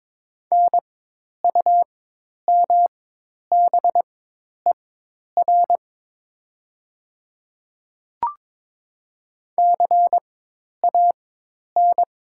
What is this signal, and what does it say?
Morse code, heard as a keyed tone, about 22 wpm, spaced as for 10 wpm, 700 Hz. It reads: NUMBER CAN
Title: 4x Character Spacing - Top 200 Words - 22wpm